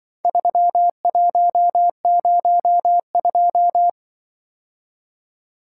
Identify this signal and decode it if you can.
Morse code, heard as a keyed tone, about 24 wpm, 700 Hz. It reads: 3102
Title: Speed-Racing - Four Letter-Number Combinations - 20wpm